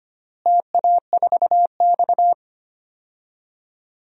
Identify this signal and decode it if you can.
Morse code, heard as a keyed tone, about 25 wpm, 700 Hz. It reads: TA4X
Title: DXCC-100 Call Signs - 4 Characters - Repeated 2x - 25wpm